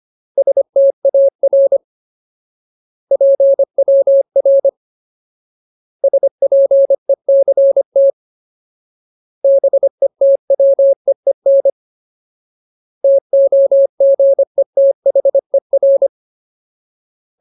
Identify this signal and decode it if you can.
Morse code, heard as a keyed tone, about 25 wpm, 550 Hz. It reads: STAR PWR SPECT BETWEEN TOGETHER
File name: Ultimate Word List / 25WPM Ultimate